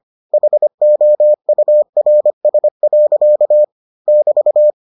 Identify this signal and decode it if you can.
Morse code, heard as a keyed tone, about 25 wpm, 600 Hz. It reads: HOURS. <BT>